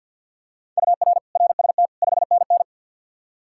Morse code, heard as a keyed tone, about 53 wpm, 700 Hz. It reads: UR RST 5NN
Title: K1USN SST QSO Practice 50wpm